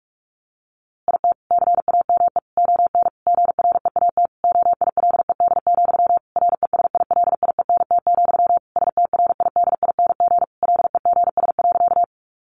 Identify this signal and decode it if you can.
Morse code, heard as a keyed tone, about 44 wpm, 700 Hz. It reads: IT CAME ON GREAT OILED, RESILIENT, STRIDING LEGS.
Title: A Sound Of Thunder 45wpm